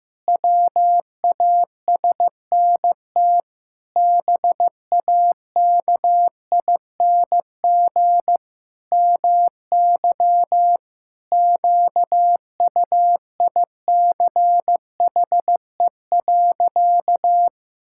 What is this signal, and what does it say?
Morse code, heard as a keyed tone, about 16 wpm, 700 Hz. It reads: WASNT BAKING MY QUICHE.
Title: Pangram Challenge v2 - 15wpm